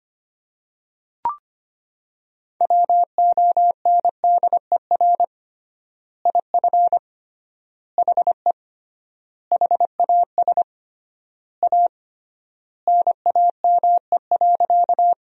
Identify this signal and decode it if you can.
Morse code, heard as a keyed tone, about 25 wpm, 700 Hz. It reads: WONDER IF HE HAS A NAME.
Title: 3x Word-Spacing - Sentences from Top 500 Words - 25wpm